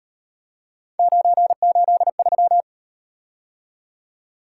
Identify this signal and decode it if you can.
Morse code, heard as a keyed tone, about 37 wpm, 700 Hz. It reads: 983